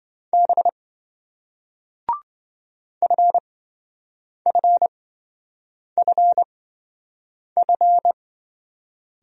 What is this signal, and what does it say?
Morse code, heard as a keyed tone, about 27 wpm, spaced as for 7 wpm, 700 Hz. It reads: B FFFF